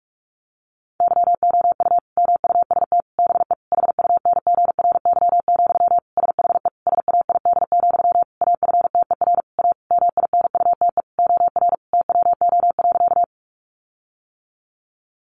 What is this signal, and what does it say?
Morse code, heard as a keyed tone, about 44 wpm, 700 Hz. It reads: YOU MUST BE HUNGRY, SHE SAID, AFTER A MINUTE OR TWO.
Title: Sentences from Top 1400 Words 45wpm